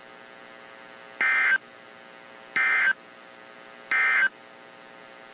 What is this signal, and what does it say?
Emergency Alert System (SAME) tones, heard as harsh buzzing bursts. It is heard from 1.1 to 4.4 s.
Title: WZ2523 - NOAA Weather Radio